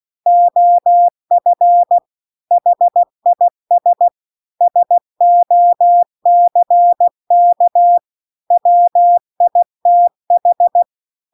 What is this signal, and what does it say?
Morse code, heard as a keyed tone, about 16 wpm, 700 Hz. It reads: OF HIS SOCK WITH